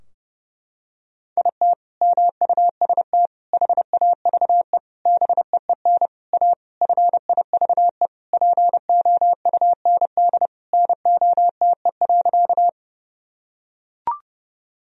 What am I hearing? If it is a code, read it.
Morse code, heard as a keyed tone, about 30 wpm, 700 Hz. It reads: IT MUST HAVE BEEN A FIVE POUND NOTE.
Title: Sentences from Top 1000 Words - Review Part 2 - 30wpm